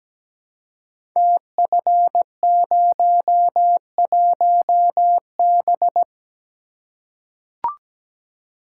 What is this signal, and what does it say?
Morse code, heard as a keyed tone, about 17 wpm, 700 Hz. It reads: TF01B
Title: Code Groups - 5 Letter:Numbers - 17wpm